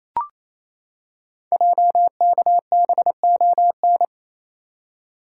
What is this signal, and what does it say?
Morse code, heard as a keyed tone, about 28 wpm, 700 Hz. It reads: JKBON
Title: Code Groups - 5 Letter:Numbers - 28wpm